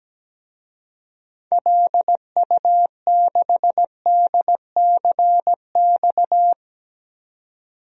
Morse code, heard as a keyed tone, about 17 wpm, 700 Hz. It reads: LU6DCX
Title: DXCC-100 Call Signs - 6 Characters - 17wpm